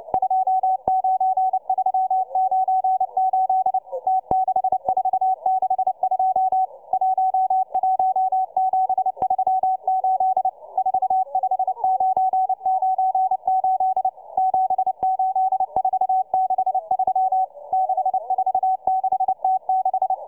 Morse code, heard as a keyed tone, about 29 wpm, 750 Hz, with radio noise. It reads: 29398 T6462 11738 45998 78463 646T6